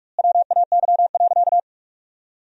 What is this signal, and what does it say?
Morse code, heard as a keyed tone, about 46 wpm, 700 Hz. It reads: WAY.
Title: 2x Word-Spacing - Sentences from Top 100 Words - 45wpm